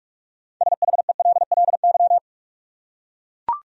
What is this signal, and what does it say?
Morse code, heard as a keyed tone, about 42 wpm, 700 Hz. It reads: SHELLY
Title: USA Common Names - 45wpm